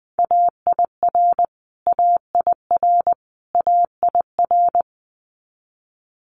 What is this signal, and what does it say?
Morse code, heard as a keyed tone, about 20 wpm, 700 Hz. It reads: AIR AIR AIR
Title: Multi-letter Phonemes Repeated 3x 20wpm